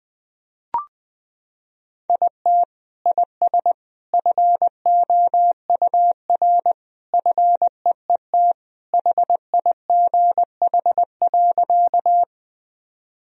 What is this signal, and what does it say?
Morse code, heard as a keyed tone, about 20 wpm, 700 Hz. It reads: IT IS FOUR FEET HIGH.
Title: Sentences from Top 1000 Words - Review Part 1 - 20wpm